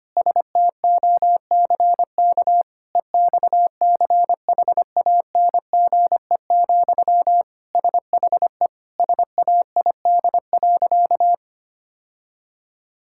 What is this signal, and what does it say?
Morse code, heard as a keyed tone, about 25 wpm, 700 Hz. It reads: STOCK EXCHANGE, SHE SAID.
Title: Sentences from Top 1000 Words - Review Part 6 - 25wpm